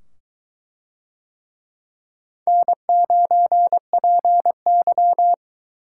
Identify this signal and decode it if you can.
Morse code, heard as a keyed tone, about 21 wpm, 700 Hz. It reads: N9PY